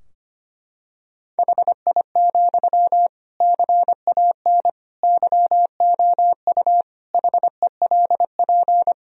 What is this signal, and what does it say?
Morse code, heard as a keyed tone, about 25 wpm, 700 Hz. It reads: HI, CAN YOU HELP
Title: Sentences from Top 1000 Words - Review Part 3 - 25wpm